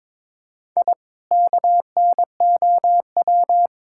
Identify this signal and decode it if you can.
Morse code, heard as a keyed tone, about 22 wpm, 700 Hz. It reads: I KNOW
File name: practice / 22wpm / Sentences with 3-Word Phrases - 22wpm